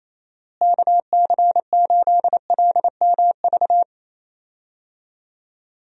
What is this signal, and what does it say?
Morse code, heard as a keyed tone, about 28 wpm, 700 Hz. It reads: KC8LMV
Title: USA Call Signs - 6 Characters - Repeated 2x - 28wpm